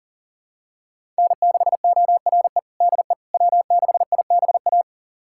Morse code, heard as a keyed tone, about 40 wpm, 700 Hz. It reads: N6ORE DE W6IBA
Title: USA Callsign DE Callsign - 5 Characters - 40wpm